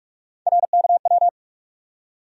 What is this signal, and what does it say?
Morse code, heard as a keyed tone, about 47 wpm, 700 Hz. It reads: RKW
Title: USA Call Sign Postfixes - 3 Letters 45wpm